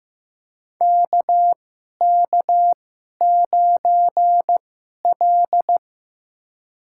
Morse code, heard as a keyed tone, about 15 wpm, spaced as for 11 wpm, 700 Hz. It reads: KK9L
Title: USA Call Signs - 4 Characters - 2x Character Spacing - 15wpm